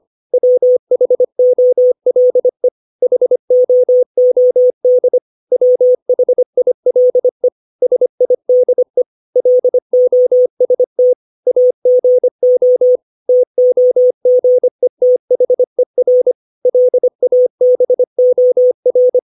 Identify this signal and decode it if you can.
Morse code, heard as a keyed tone, about 25 wpm, 500 Hz. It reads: WHOLE HOOD WHILE SIDE LOST AGO TOGETHER LABOR
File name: 632 words / 632 Words 25WPM